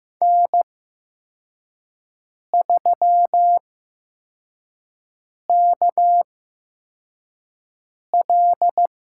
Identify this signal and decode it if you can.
Morse code, heard as a keyed tone, about 15 wpm, spaced as for 4 wpm, 700 Hz. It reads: N3KL
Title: USA Call Signs - 4 Characters - 8x Character Spacing - 15wpm